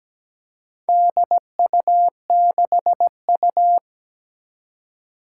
Morse code, heard as a keyed tone, about 17 wpm, 700 Hz. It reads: DU6U
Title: Four Challenging Letter-Number Combinations 17wpm